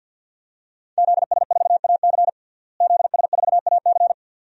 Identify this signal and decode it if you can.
Morse code, heard as a keyed tone, about 46 wpm, 700 Hz. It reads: ZS4AC ZS4AC
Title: DXCC-100 Call Signs - 5 Characters - Repeated 2x - 50wpm